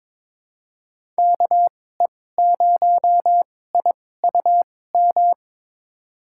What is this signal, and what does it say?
Morse code, heard as a keyed tone, about 22 wpm, spaced as for 16 wpm, 700 Hz. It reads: KE0IUM
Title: USA Call Signs - 2x Character Spacing - 22wpm